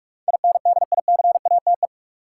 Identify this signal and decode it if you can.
Morse code, heard as a keyed tone, about 46 wpm, 700 Hz. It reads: INDICATE
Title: Sentences from Top 1300 Words - New Word Drill - 45wpm